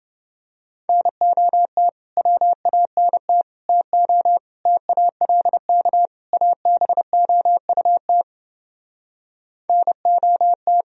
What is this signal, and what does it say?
Morse code, heard as a keyed tone, about 30 wpm, 700 Hz. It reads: NOT WANT TO TALK ABOUT NOT